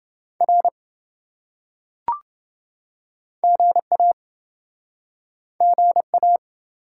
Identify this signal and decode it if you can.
Morse code, heard as a keyed tone, about 28 wpm, 700 Hz. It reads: R GA GA